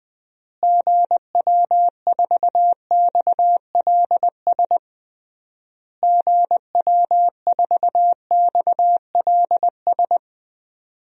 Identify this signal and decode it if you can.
Morse code, heard as a keyed tone, about 20 wpm, 700 Hz. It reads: GW4XLS GW4XLS
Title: DXCC-100 Call Signs - 6 Characters - Repeated 3x - 20wpm